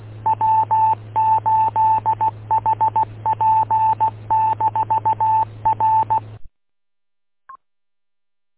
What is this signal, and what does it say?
Morse code, heard as a keyed tone, about 16 wpm, 880 Hz, with radio noise. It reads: W8HP-R